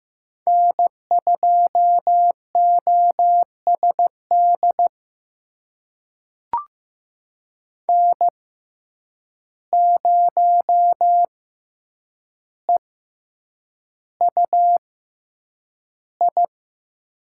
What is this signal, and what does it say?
Morse code, heard as a keyed tone, about 15 wpm, spaced as for 8 wpm, 700 Hz. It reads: N2OSD N0EUI
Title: USA Call Signs - 5 Characters - 6x Character Spacing - 15wpm